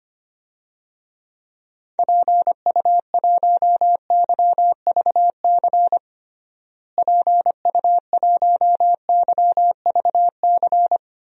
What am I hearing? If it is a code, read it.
Morse code, heard as a keyed tone, about 25 wpm, 700 Hz. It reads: PU1YVC PU1YVC
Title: DXCC-100 Call Signs - 6 Characters - Repeated 3x - 25wpm